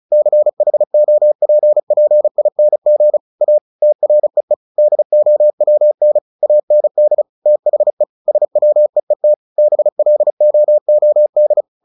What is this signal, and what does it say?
Morse code, heard as a keyed tone, about 34 wpm, 600 Hz. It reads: CHOPPING A TREE DOWN AND THE SWEET BLOOD
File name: the old man and the sea / the old man and the sea 1 35wpm